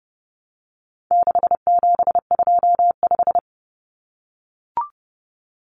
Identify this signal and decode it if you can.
Morse code, heard as a keyed tone, about 28 wpm, 700 Hz. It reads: 6725
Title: Numbers - 4-Digit 30wpm